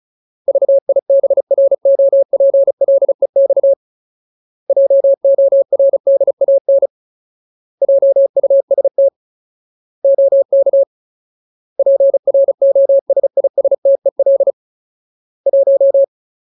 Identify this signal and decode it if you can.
Morse code, heard as a keyed tone, about 34 wpm, 550 Hz. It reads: VIBROPLEX JORDAN JUST OK PROSISTEL 1
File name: Ultimate Word List / 35WPM Ultimate